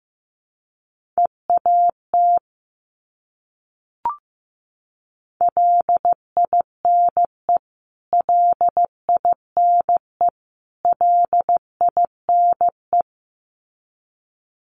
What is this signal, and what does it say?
Morse code, heard as a keyed tone, about 15 wpm, 700 Hz. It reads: EAT LINE LINE LINE
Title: Top 500 Words Repeated 15wpm